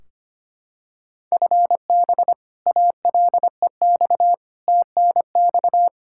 Morse code, heard as a keyed tone, about 25 wpm, 700 Hz. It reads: FB ALEX TNX
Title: Practice QSO - Exchange 3 - More Info - 25wpm